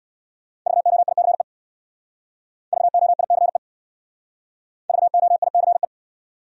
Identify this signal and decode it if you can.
Morse code, heard as a keyed tone, about 65 wpm, 700 Hz. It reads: VOICE VOICE VOICE